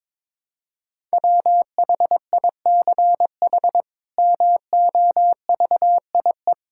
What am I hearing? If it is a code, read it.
Morse code, heard as a keyed tone, about 22 wpm, 700 Hz. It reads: WHICH MOVIE
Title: Sentences from Top 1000 Words - Review Part 5 - 22wpm